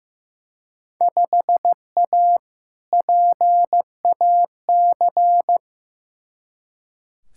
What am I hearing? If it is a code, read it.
Morse code, heard as a keyed tone, about 15 wpm, 700 Hz. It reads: 5A PAC
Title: Field Day Exchanges Encore - 15wpm